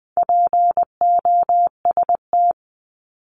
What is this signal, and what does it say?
Morse code, heard as a keyed tone, about 20 wpm, 700 Hz. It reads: POST
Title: 260 Lesson 42e - Words 20-20wpm